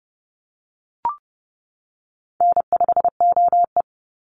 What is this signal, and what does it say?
Morse code, heard as a keyed tone, about 30 wpm, 700 Hz. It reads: N5OE
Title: Easy Call Signs - 30wpm